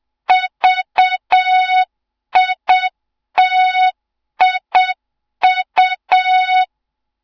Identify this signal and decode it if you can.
Morse code, heard as a keyed tone, about 7 wpm, 3700 Hz. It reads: VITIU
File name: Morze-HA2MN / hang / 07 GYAKORLAT